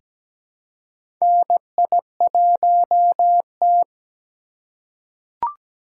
Morse code, heard as a keyed tone, about 17 wpm, 700 Hz. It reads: NI1T
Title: USA Call Signs - 3x Character Spacing - 17wpm